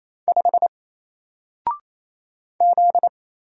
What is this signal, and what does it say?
Morse code, heard as a keyed tone, about 27 wpm, 700 Hz. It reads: 5 Z
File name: Single-Letter-Number-28wpm